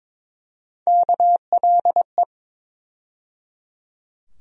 Morse code, heard as a keyed tone, about 22 wpm, 700 Hz. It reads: KLE
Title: USA Call Sign Postfixes - 3 Letters 22wpm